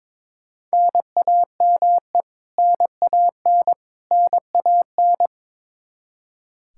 Morse code, heard as a keyed tone, about 22 wpm, 700 Hz. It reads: NAME NAN NAN
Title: K1USN SST QSO Practice 22wpm